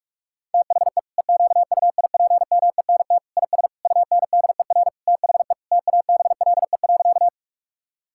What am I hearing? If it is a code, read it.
Morse code, heard as a keyed tone, about 41 wpm, 700 Hz. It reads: THE EQUIPMENT IS UNDER THE TABLE.
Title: Sentences from Top 1000 Words - Review Part 5 - 45wpm